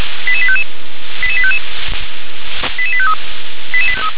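Five-tone selective call (ZVEI), heard as quick runs of five tones, tones of about 70 ms. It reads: 88855 88855 88853
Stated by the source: Five-tone ZVEI selective call received on SDR.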